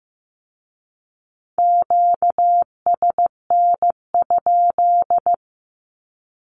Morse code, heard as a keyed tone, about 15 wpm, 700 Hz. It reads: QSN?
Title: Q-Codes with Spoken Meaning - 15wpm